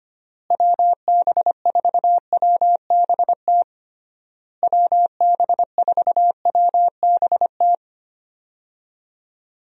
Morse code, heard as a keyed tone, about 25 wpm, 700 Hz. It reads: WB4WBT WB4WBT
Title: USA Call Signs - 6 Characters - Repeated 3x - 25wpm